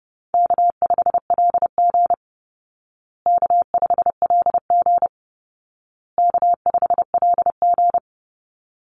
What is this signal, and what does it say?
Morse code, heard as a keyed tone, about 30 wpm, 700 Hz. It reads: K5LG K5LG K5LG
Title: Easy Call Signs Repeated 3x - 30wpm